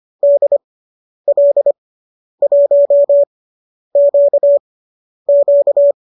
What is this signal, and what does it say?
Morse code, heard as a keyed tone, about 25 wpm, spaced as for 10 wpm, 590 Hz. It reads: DL1QQ
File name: qso219 10